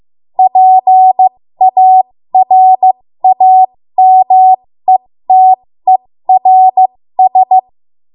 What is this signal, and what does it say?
Morse code, heard as a keyed tone, about 15 wpm, spaced as for 13 wpm, 750 Hz. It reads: PARAMETERS
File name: Morse / 13WPM